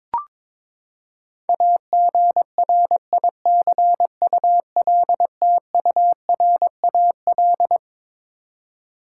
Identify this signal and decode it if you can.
Morse code, heard as a keyed tone, about 22 wpm, 700 Hz. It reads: AGRICULTURAL
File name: Twelve Letter Words - 22wpm